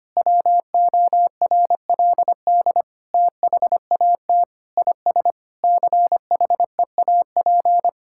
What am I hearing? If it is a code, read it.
Morse code, heard as a keyed tone, about 25 wpm, 700 Hz. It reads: WORLD THAT IS CHEAP
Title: Sentences from Top 1600 Words 25wpm